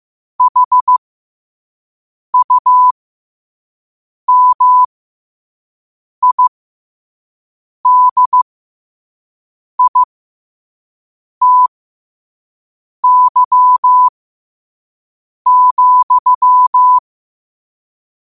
Morse code, heard as a keyed tone, about 15 wpm, spaced as for 5 wpm, 1000 Hz. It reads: HUMIDITY,
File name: qotdfiles / 5wpm